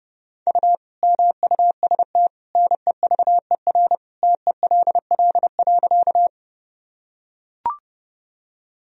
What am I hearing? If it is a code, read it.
Morse code, heard as a keyed tone, about 30 wpm, 700 Hz. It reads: U MUST NEVER TELL.
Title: Sentences from Top 300 Words and Abbreviations 30wpm